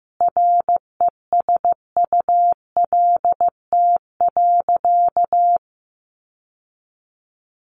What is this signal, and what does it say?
Morse code, heard as a keyed tone, about 14 wpm, 700 Hz. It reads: RESULT.
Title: Sentences from Top 800 Words 15wpm